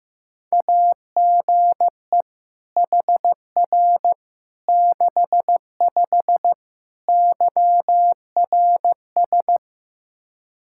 Morse code, heard as a keyed tone, about 15 wpm, 700 Hz. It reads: AGE HR 65 YRS
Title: Practice QSO - Exchange 3 - More Info - 15wpm